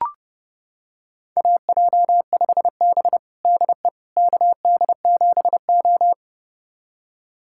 Morse code, heard as a keyed tone, about 30 wpm, 700 Hz. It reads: AJ5B DE KD7O
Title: USA Callsign DE Callsign - 4 Characters - 30wpm